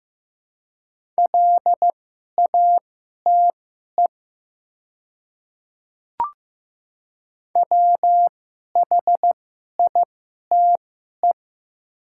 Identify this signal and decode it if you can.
Morse code, heard as a keyed tone, about 15 wpm, spaced as for 11 wpm, 690 Hz. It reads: LATE WHITE
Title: 2x Character Spacing - Top 500 Words - 15wpm